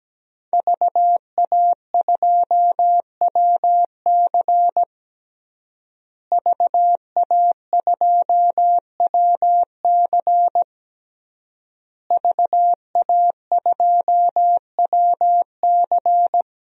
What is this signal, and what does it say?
Morse code, heard as a keyed tone, about 17 wpm, 700 Hz. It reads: VA2WC VA2WC VA2WC
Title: DXCC-100 Call Signs - 5 Characters - Repeated 3x - 17wpm